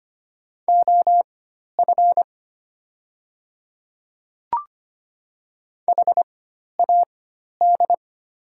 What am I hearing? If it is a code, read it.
Morse code, heard as a keyed tone, about 25 wpm, spaced as for 12 wpm, 700 Hz. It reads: OF HAD
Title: 4x Character Spacing - Top 200 Words - 25wpm